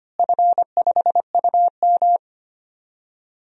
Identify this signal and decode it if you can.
Morse code, heard as a keyed tone, about 25 wpm, 700 Hz. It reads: F5UM
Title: DXCC-100 Call Signs - 4 Characters - Repeated 2x - 25wpm